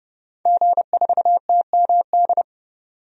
Morse code, heard as a keyed tone, about 30 wpm, 700 Hz. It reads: G4TMD
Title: DXCC-100 Call Signs - 5 Characters - 30wpm